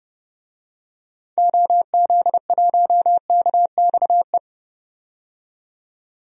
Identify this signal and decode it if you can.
Morse code, heard as a keyed tone, about 30 wpm, 700 Hz. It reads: OZ1KXE
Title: DXCC-100 Call Signs - 6 Characters - 30wpm